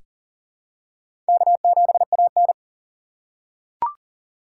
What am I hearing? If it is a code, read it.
Morse code, heard as a keyed tone, about 40 wpm, 700 Hz. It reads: K7AN
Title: USA Call Signs - 2x Character Spacing - 40wpm